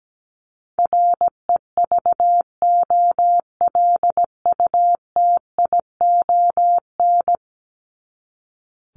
Morse code, heard as a keyed tone, about 17 wpm, 700 Hz. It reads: REVOLUTION
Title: Top 1751-2000 Words 17wpm